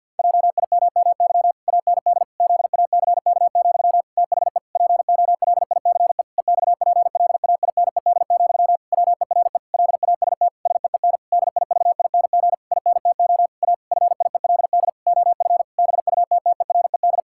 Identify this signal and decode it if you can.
Morse code, heard as a keyed tone, about 51 wpm, 700 Hz. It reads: JIMMY AND ZACK, THE POLICE EXPLAINED, WERE LAST SEEN DIVING INTO A FIELD OF BUTTERED
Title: Pangram Challenge v2 - 50wpm